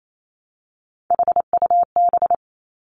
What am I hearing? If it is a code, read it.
Morse code, heard as a keyed tone, about 28 wpm, 700 Hz. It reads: HUB
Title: Three Challenging Letter-Number Combinations 28wpm